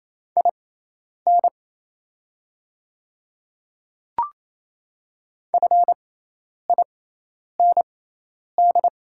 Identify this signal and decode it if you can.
Morse code, heard as a keyed tone, about 28 wpm, spaced as for 10 wpm, 690 Hz. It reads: IN FIND
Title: 6x Character Spacing - Top 100 Words - 28wpm